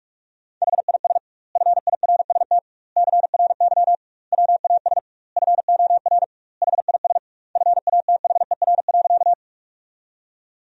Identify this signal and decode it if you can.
Morse code, heard as a keyed tone, about 48 wpm, 700 Hz. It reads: HIS FIRST CRY WAS FOR HIS FATHER.
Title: 2x Word-Spacing - Sentences from Top 500 Words - 45wpm